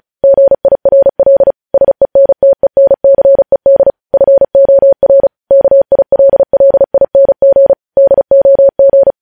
Morse code, heard as a keyed tone, about 36 wpm, 550 Hz. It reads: GIRL SENTENCED FOR KILLING DOG
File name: podcast / News headlines-1x-35x35